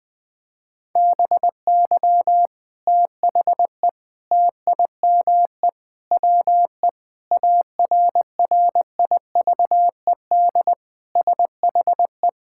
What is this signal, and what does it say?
Morse code, heard as a keyed tone, about 20 wpm, 700 Hz. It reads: BY THE TIME WE ARRIVED SHE
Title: Sentences with 3-Word Phrases - Spoken Beginning - 20wpm